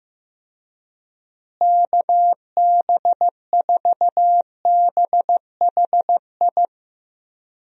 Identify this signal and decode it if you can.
Morse code, heard as a keyed tone, about 16 wpm, 700 Hz. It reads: KB4BHI
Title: USA Call Signs - 6 Characters - Repeated 2x - 15wpm